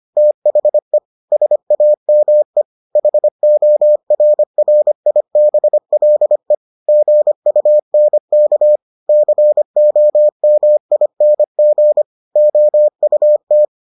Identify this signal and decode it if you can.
Morse code, heard as a keyed tone, about 25 wpm, 600 Hz. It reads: THE SAME HORRIBLE GUNK COMING OUT